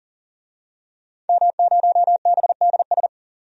Morse code, heard as a keyed tone, about 40 wpm, 700 Hz. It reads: M0BDS
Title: DXCC-100 Call Signs - 5 Characters - 40wpm